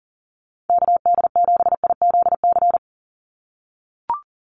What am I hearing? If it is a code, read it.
Morse code, heard as a keyed tone, about 39 wpm, 700 Hz. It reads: KD7IZC
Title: USA Call Signs 40wpm